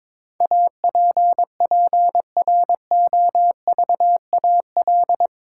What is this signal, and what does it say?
Morse code, heard as a keyed tone, about 21 wpm, 700 Hz. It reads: APPROVAL
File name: practice / 22wpm / Eight Letter Words - 22wpm